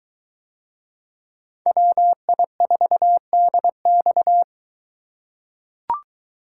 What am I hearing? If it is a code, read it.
Morse code, heard as a keyed tone, about 23 wpm, 700 Hz. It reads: WI4DX